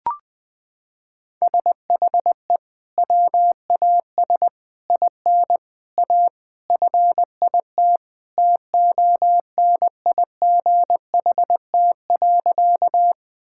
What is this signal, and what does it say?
Morse code, heard as a keyed tone, about 21 wpm, 700 Hz. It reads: SHE WAS IN A FIT TONIGHT.
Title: Sentences from Top 1000 Words - Review Part 5 - 20wpm